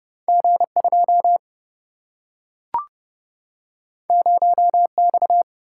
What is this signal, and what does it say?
Morse code, heard as a keyed tone, about 31 wpm, 700 Hz. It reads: G 2 0 X
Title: Single-Letter Single-Number 30wpm